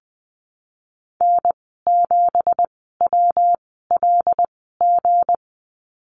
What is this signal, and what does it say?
Morse code, heard as a keyed tone, about 20 wpm, spaced as for 15 wpm, 700 Hz. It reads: N7WLG
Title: 186 Lesson 30g - Letter b - Call Signs 20wpm